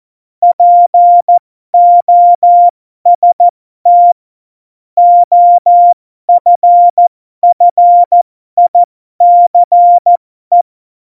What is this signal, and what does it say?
Morse code, heard as a keyed tone, about 14 wpm, spaced as for 12 wpm, 700 Hz. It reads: POST OFFICE
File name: Morsekurs / Witze